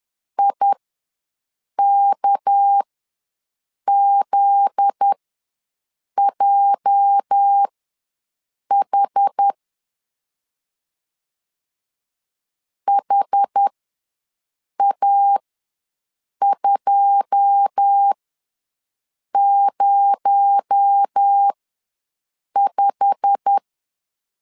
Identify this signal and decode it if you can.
Morse code, heard as a keyed tone, about 10 wpm, spaced as for 6 wpm, 800 Hz. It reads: IKZJH HA205
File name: LEZIONI / 9 2 ws 10wpm 4130